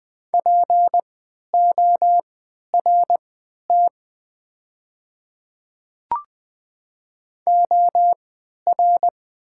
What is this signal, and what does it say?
Morse code, heard as a keyed tone, about 20 wpm, spaced as for 11 wpm, 700 Hz. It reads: PORT OR
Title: 3x Character Spacing - Top 200 Words - 20wpm